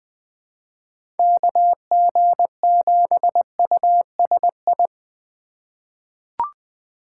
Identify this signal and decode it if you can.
Morse code, heard as a keyed tone, about 20 wpm, 700 Hz. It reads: KG7USI
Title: USA Call Signs - 8x Character Spacing - 20wpm